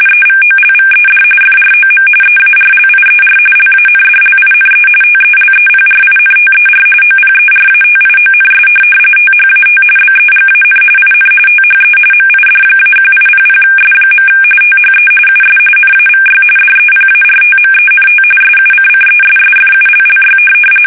RTTY, about 45.45 baud. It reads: EVOLVED FROM EARLIER LANDLINE TELEPRINTER OPERATIONS THAT BEGAN IN THE MID-1800S. THE US NAVY DEPARTMENT SUCCESSFULLY
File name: fsk850 / teletype-RTTY